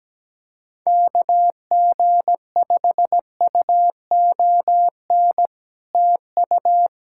Morse code, heard as a keyed tone, about 17 wpm, 700 Hz. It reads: KG5UON TU
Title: Speed-Racing Common QSO Phrases - 17wpm